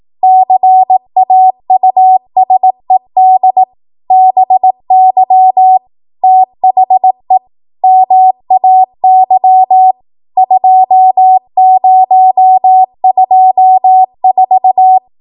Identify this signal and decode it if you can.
Morse code, heard as a keyed tone, about 18 wpm, 750 Hz. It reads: CAUSED BY THE MAY 2024